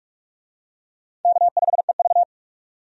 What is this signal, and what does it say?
Morse code, heard as a keyed tone, about 43 wpm, 700 Hz. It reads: K5EV
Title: USA Call Signs - 4 Characters - 45wpm